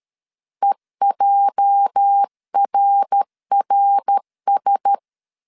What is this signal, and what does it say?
Morse code, heard as a keyed tone, about 13 wpm, 800 Hz. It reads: EJRRS
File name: CW / DET / 1 9 1500C S C 12wpm 3811